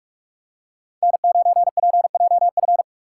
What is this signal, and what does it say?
Morse code, heard as a keyed tone, about 44 wpm, 700 Hz. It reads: N9PJF